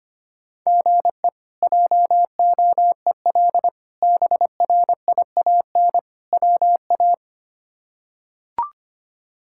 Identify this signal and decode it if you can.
Morse code, heard as a keyed tone, about 25 wpm, 700 Hz. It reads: GE JOEL BRIAN WA
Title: K1USN SST QSO Practice - 25wpm